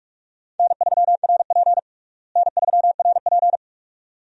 Morse code, heard as a keyed tone, about 43 wpm, 700 Hz. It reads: N3RP N3RP
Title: USA Call Signs - 4 Characters - Repeated 2x - 45wpm